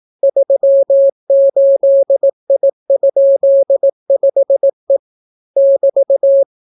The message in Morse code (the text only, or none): 38I?5E <BT>